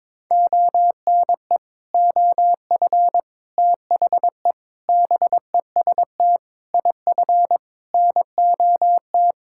Morse code, heard as a keyed tone, about 22 wpm, 700 Hz. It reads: ONE OF THE BEST IF NOT
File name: practice / 22wpm / Sentences with 8-Word Phrases - 22wpm